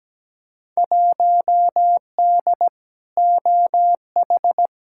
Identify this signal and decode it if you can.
Morse code, heard as a keyed tone, about 17 wpm, 700 Hz. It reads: 1D OH
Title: Field Day Exchanges - 17wpm